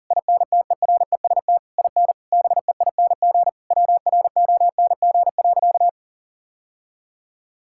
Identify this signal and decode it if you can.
Morse code, heard as a keyed tone, about 40 wpm, 700 Hz. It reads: INTEREST IN BEING WRONG.
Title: Sentences from Top 1000 Words - Review Part 6 - 40wpm